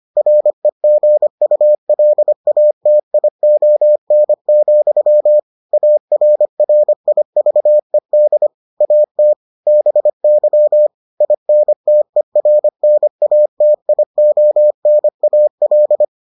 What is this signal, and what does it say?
Morse code, heard as a keyed tone, about 25 wpm, 600 Hz. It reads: REGULATION, ARRIVED AT BY INTERNATIONAL